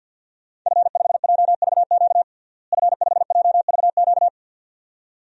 Morse code, heard as a keyed tone, about 47 wpm, 700 Hz. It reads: F5JVQ F5JVQ
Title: DXCC-100 Call Signs - 5 Characters - Repeated 2x - 50wpm